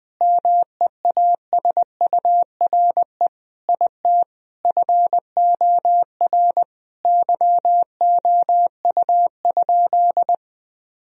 Morse code, heard as a keyed tone, about 21 wpm, 700 Hz. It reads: MEASURE IT FOR YOU?
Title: Sentences from Top 1000 Words - Review Part 2 - 20wpm